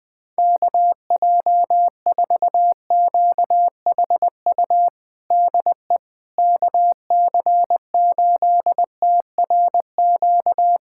Morse code, heard as a keyed tone, about 21 wpm, 700 Hz. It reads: KJ4QHU DE KC8TRQ
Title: USA Callsign DE Callsign 20wpm